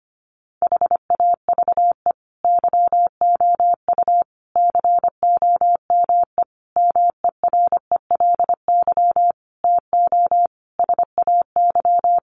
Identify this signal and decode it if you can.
Morse code, heard as a keyed tone, about 25 wpm, 700 Hz. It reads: HAVE YOU COME MERELY TO SAY
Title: Sentences from Top 1400 Words 25wpm